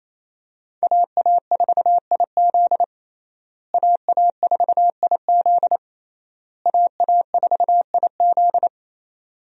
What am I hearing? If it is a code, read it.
Morse code, heard as a keyed tone, about 28 wpm, 700 Hz. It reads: AA4IZ AA4IZ AA4IZ
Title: USA Call Signs - 5 Characters - Repeated 3x - 28wpm